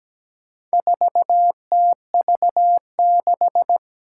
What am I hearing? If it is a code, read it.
Morse code, heard as a keyed tone, about 17 wpm, 700 Hz. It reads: 4TV6
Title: Four Challenging Letter-Number Combinations 17wpm